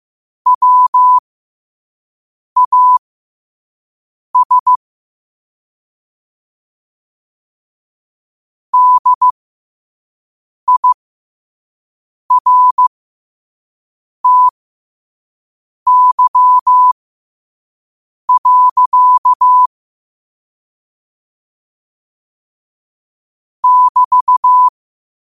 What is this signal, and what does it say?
Morse code, heard as a keyed tone, about 15 wpm, spaced as for 5 wpm, 1000 Hz. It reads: WAS DIRTY. <BT>
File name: qotdfiles / 5wpm